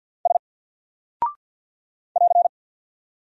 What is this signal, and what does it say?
Morse code, heard as a keyed tone, about 50 wpm, 700 Hz. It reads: S <AR>
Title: Single Character (Letter:Number:Punctuation:Prosign) 50wpm